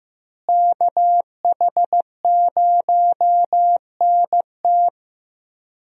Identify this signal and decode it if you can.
Morse code, heard as a keyed tone, about 15 wpm, 700 Hz. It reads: KH0NT
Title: USA Call Signs - 5 Characters - Repeated 2x - 15wpm